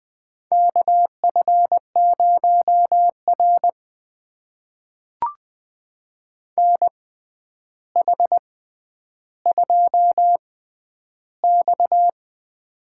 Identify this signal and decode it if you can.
Morse code, heard as a keyed tone, about 20 wpm, spaced as for 10 wpm, 700 Hz. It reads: KF0R NH2X
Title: USA Call Signs - 4 Characters - 6x Character Spacing - 20wpm